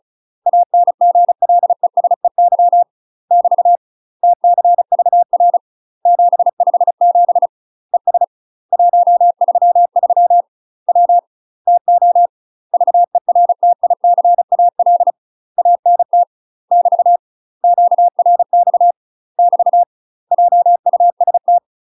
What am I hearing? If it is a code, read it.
Morse code, heard as a keyed tone, about 36 wpm, 700 Hz. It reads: ANGLESEY <BT> TCVR 757 ES 133 W TO VERTICAL ANT <BT> QRX <BT> JUST
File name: UK MASTER QSOs TRAINING / 35WPM / UK MASTER QSOs 0016